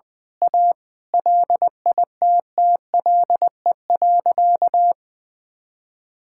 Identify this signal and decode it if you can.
Morse code, heard as a keyed tone, about 20 wpm, 700 Hz. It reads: A LITTLE.
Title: Sentences from Top 1000 Words - Review Part 5 - 20wpm